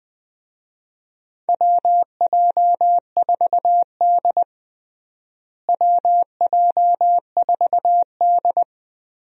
Morse code, heard as a keyed tone, about 20 wpm, 700 Hz. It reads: WJ4D WJ4D
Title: USA Call Signs - 4 Characters - Repeated 3x - 20wpm